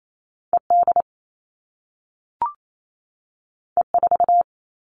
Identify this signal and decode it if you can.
Morse code, heard as a keyed tone, about 28 wpm, 700 Hz. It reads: ED E4
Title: All Cut Numbers - 2-Digit 28wpm